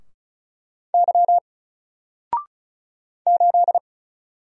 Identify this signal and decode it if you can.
Morse code, heard as a keyed tone, about 34 wpm, 700 Hz. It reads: Y 8
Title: Single Letter-Number Alt 35wpm